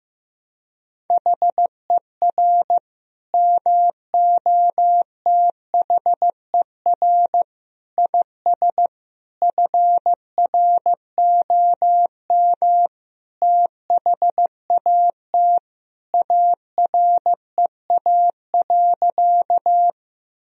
Morse code, heard as a keyed tone, about 15 wpm, 700 Hz. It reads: HER MOTHER IS FROM THAT AREA.
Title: Sentences from Top 1000 Words - Review Part 2 - 15wpm